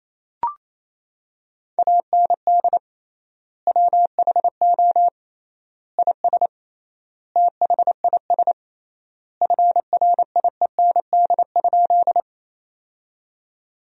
Morse code, heard as a keyed tone, about 27 wpm, 700 Hz. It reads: AND WHO IS THIS FRIEND?
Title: 3x Word-Spacing - Sentences from Top 300 Words - 28wpm